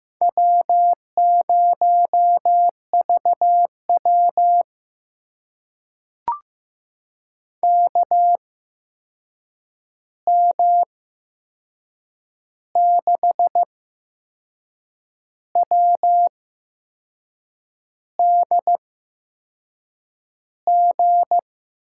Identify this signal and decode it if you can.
Morse code, heard as a keyed tone, about 15 wpm, spaced as for 6 wpm, 700 Hz. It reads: W0VW KM6WDG
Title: USA Call Signs - 8x Character Spacing - 15wpm